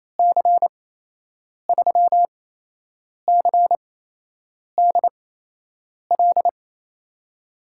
Morse code, heard as a keyed tone, about 28 wpm, spaced as for 8 wpm, 700 Hz. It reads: C3CDL
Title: USA Call Signs - 6 Characters - 8x Character Spacing - 28wpm